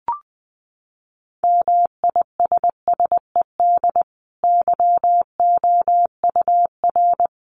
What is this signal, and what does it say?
Morse code, heard as a keyed tone, about 20 wpm, 700 Hz. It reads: MISSED YOUR
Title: Common QSO Phrases 20wpm